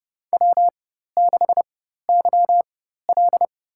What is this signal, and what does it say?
Morse code, heard as a keyed tone, about 29 wpm, spaced as for 14 wpm, 700 Hz. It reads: W6YL
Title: USA Call Signs - 4 Characters - 4x Character Spacing - 30wpm